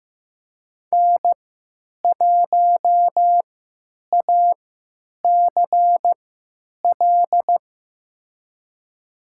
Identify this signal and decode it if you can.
Morse code, heard as a keyed tone, about 15 wpm, spaced as for 9 wpm, 700 Hz. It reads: N1ACL
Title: USA Call Signs - 5 Characters - 3x Character Spacing - 15wpm